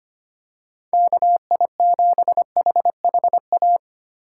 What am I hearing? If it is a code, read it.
Morse code, heard as a keyed tone, about 25 wpm, 700 Hz. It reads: KI7HHA
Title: USA Call Signs - 2x Character Spacing - 25wpm